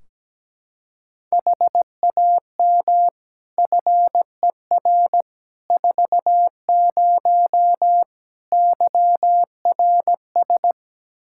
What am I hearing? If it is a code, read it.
Morse code, heard as a keyed tone, about 17 wpm, 700 Hz. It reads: HAM FER 40 YRS
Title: Practice QSO - Exchange 3 - More Info - 17wpm